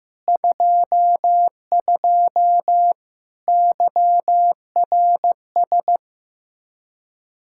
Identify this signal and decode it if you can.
Morse code, heard as a keyed tone, about 16 wpm, 700 Hz. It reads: 22 YRS
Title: Practice QSO - Exchange 3 - More Info - 15wpm